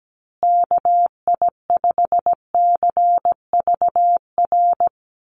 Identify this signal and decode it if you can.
Morse code, heard as a keyed tone, about 17 wpm, 700 Hz. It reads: KI5CVR
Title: USA Call Signs Repeated 2x 17wpm